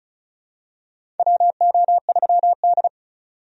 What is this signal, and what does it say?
Morse code, heard as a keyed tone, about 34 wpm, 700 Hz. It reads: WO3D
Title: USA Call Signs - 4 Characters - 35wpm